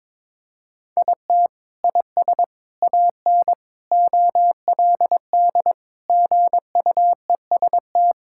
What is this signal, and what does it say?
Morse code, heard as a keyed tone, about 21 wpm, 700 Hz. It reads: IT IS AN OLD GUEST
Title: Sentences from Top 1600 Words 22wpm